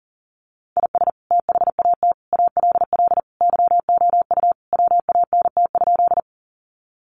Morse code, heard as a keyed tone, about 40 wpm, 700 Hz. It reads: IS THAT ALL YOU WANT?
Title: Sentences from Top 200 Words 40wpm